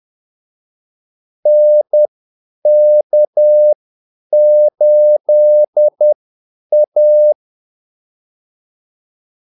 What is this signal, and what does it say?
Morse code, heard as a keyed tone, about 10 wpm, spaced as for 8 wpm, 600 Hz. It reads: NK8A
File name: Callsigns-10-8wpm